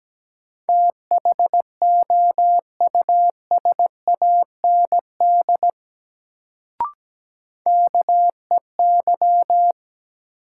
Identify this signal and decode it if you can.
Morse code, heard as a keyed tone, about 17 wpm, 700 Hz. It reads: THOUSAND KEY
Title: Sentences from Top 600 Words - New Word Drill - 17wpm